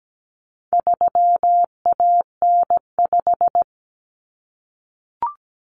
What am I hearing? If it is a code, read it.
Morse code, heard as a keyed tone, about 17 wpm, 700 Hz. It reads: 3AN5
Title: Common Cut Numbers - 4-Digit 17wpm